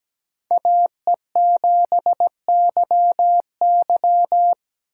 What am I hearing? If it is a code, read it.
Morse code, heard as a keyed tone, about 17 wpm, 700 Hz. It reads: AE7YY
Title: USA Call Signs - 5 Characters - 3x Character Spacing - 17wpm